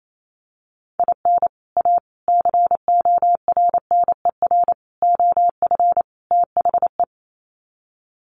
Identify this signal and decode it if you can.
Morse code, heard as a keyed tone, about 28 wpm, 700 Hz. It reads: IN A CORNER OF THE
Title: Sets of 5 Words Encore2 28wpm